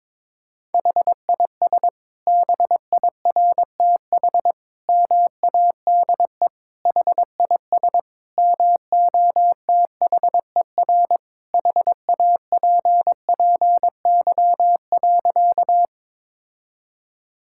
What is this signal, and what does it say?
Morse code, heard as a keyed tone, about 21 wpm, 700 Hz. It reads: HIS BIRTH MADE HIS MOTHER HAPPY.
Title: Sentences from Top 1000 Words - Review Part 5 - 22wpm